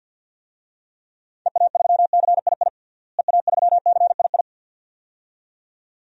Morse code, heard as a keyed tone, about 52 wpm, 700 Hz. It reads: EA3CII EA3CII
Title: DXCC-100 Call Signs - 6 Characters - Repeated 2x - 50wpm